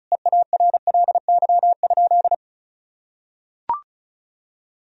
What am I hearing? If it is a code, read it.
Morse code, heard as a keyed tone, about 36 wpm, 700 Hz. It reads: EARLY?
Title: Sentences from Top 400 Words and Abbreviations 35wpm